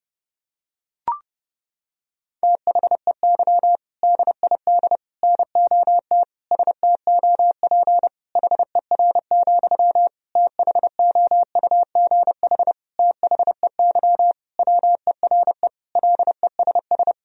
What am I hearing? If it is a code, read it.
Morse code, heard as a keyed tone, about 30 wpm, 700 Hz. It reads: THEY DID NOT STOP HER, THOUGH THEY WERE LESS
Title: Sentences from Top 1000 Words - Review Part 2 - 30wpm